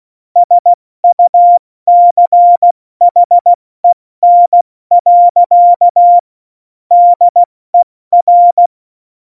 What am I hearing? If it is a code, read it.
Morse code, heard as a keyed tone, about 16 wpm, 700 Hz. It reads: SUCHEN. DER
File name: Morsekurs / Witze / Witz 29. Dezember 2020, 70 F80